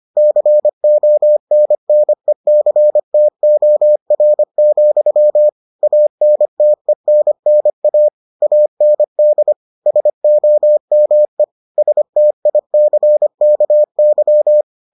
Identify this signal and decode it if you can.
Morse code, heard as a keyed tone, about 25 wpm, 600 Hz. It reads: CONNECTOR, ANTENNA AND SOME STICKY